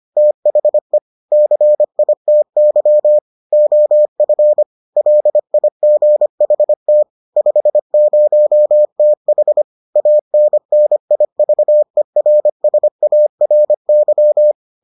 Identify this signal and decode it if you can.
Morse code, heard as a keyed tone, about 25 wpm, 600 Hz. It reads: THE CITY OF LIGHT 50TH ANNIVERSARY